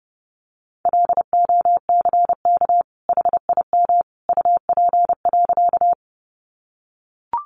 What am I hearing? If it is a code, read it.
Morse code, heard as a keyed tone, about 30 wpm, 700 Hz. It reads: LOCK HIM UP.
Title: Sentences from Top 1300 Words 30wpm